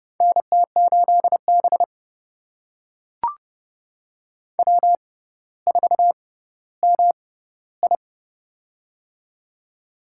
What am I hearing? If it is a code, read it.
Morse code, heard as a keyed tone, about 29 wpm, spaced as for 15 wpm, 700 Hz. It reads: NT8B W4MI
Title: USA Call Signs - 4 Characters - 6x Character Spacing - 30wpm